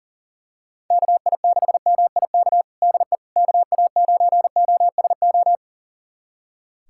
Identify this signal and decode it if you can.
Morse code, heard as a keyed tone, about 40 wpm, 700 Hz. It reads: KI6MIK DE KA9OSO
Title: USA Callsign DE Callsign - 6 Characters - 40wpm